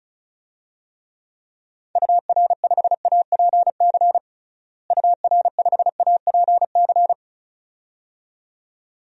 Morse code, heard as a keyed tone, about 36 wpm, 700 Hz. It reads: UR5APC UR5APC
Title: DXCC-100 Call Signs - 6 Characters - Repeated 2x - 35wpm